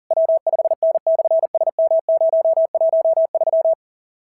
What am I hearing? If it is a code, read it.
Morse code, heard as a keyed tone, about 40 wpm, 690 Hz. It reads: W5N/SM013
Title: SOTA QSO Practice 40wpm